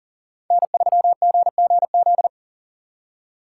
Morse code, heard as a keyed tone, about 48 wpm, 700 Hz. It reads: N3GGZ
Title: Speed Racing - USA 1x3 Call Signs 40wpm